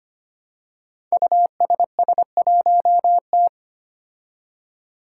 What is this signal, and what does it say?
Morse code, heard as a keyed tone, about 25 wpm, 700 Hz. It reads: USS1T
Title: Five Challenging Letter-Number Combinations 25wpm